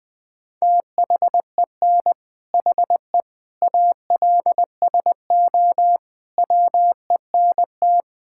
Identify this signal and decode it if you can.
Morse code, heard as a keyed tone, about 21 wpm, 700 Hz. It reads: THEN HE ALSO WENT
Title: Sentences from Top 1600 Words 20wpm